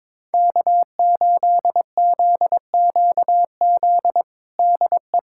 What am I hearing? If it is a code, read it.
Morse code, heard as a keyed tone, about 22 wpm, 700 Hz. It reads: K8ZQZ DE
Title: USA Callsign DE Callsign - 5 Characters - 22wpm